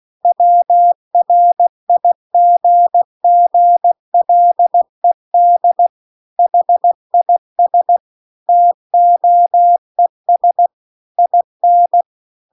Morse code, heard as a keyed tone, about 16 wpm, 700 Hz. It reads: WRIGGLED HIS TOES IN
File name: CW-The-Big-Twoheaded-River-2-16WPM